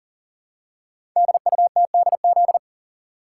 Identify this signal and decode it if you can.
Morse code, heard as a keyed tone, about 42 wpm, 700 Hz. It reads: DUTDZ
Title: Code Groups - 5 Letters - 40wpm